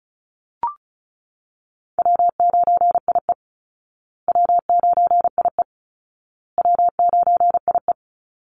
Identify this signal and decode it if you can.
Morse code, heard as a keyed tone, about 34 wpm, 700 Hz. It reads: W9IE W9IE W9IE
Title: Call Signs Repeated 2x - 35wpm